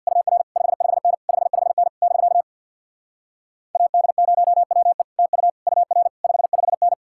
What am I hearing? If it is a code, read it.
Morse code, heard as a keyed tone, about 66 wpm, 700 Hz. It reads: UR 55N 55N <BK> AD0WE TU UR 55N